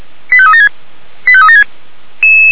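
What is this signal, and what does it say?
Five-tone selective call (ZVEI), heard as quick runs of five tones, tones of about 70 ms. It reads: 85376 85376
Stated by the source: ZVEI five-tone call 89337 received by SDR.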